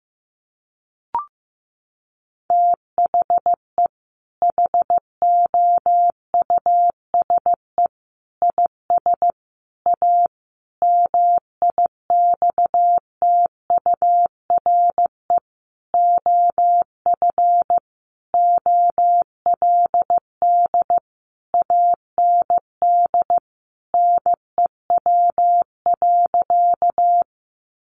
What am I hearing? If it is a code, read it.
Morse code, heard as a keyed tone, about 15 wpm, 700 Hz. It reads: THE HOUSE IS A MIXTURE OF OLD AND NEW.
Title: Sentences from Top 1500 Words 15wpm